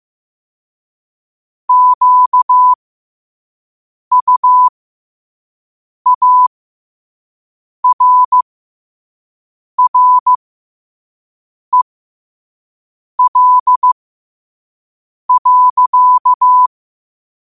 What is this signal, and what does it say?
Morse code, heard as a keyed tone, about 15 wpm, spaced as for 5 wpm, 1000 Hz. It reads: QUARREL.
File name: qotdfiles / 5wpm